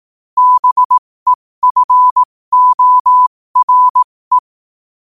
Morse code, heard as a keyed tone, about 18 wpm, spaced as for 15 wpm, 1000 Hz. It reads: BEFORE
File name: qotdfiles / 15wpm